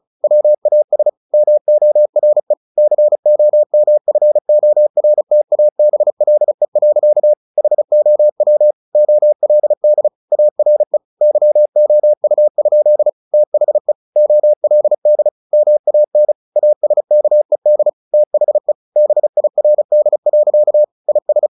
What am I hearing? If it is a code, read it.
Morse code, heard as a keyed tone, about 34 wpm, 600 Hz. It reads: WAS MORE COMFORTABLE. HOW OLD ARE YOU? THE OLD MAN ASKED THE BIRD. IS
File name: the old man and the sea / the old man and the sea 1 35wpm